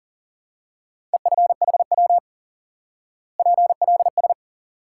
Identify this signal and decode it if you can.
Morse code, heard as a keyed tone, about 40 wpm, 700 Hz. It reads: EFHW PLS
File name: IQE Course 3 - Lesson 89 - YAGI - Two Elements - 40wpm